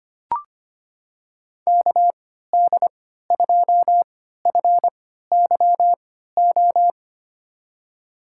Morse code, heard as a keyed tone, about 25 wpm, spaced as for 14 wpm, 700 Hz. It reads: KD2FYO
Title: USA Call Signs - 6 Characters - 3x Character Spacing - 25wpm